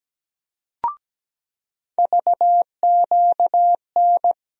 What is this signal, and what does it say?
Morse code, heard as a keyed tone, about 17 wpm, 700 Hz. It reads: VQN
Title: USA Call Sign Postfixes - 3 Letters 17wpm